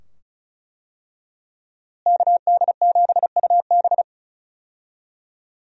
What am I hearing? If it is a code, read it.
Morse code, heard as a keyed tone, about 34 wpm, 700 Hz. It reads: KD7UB
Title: USA Call Signs - 5 Characters - Repeated 2x - 35wpm